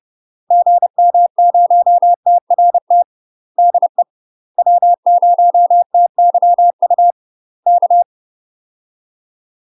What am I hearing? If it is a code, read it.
Morse code, heard as a keyed tone, about 30 wpm, 700 Hz. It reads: GM0TRT DE W0TYU K
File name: UK MASTER QSOs TRAINING / 30WPM / UK MASTER QSOs 0009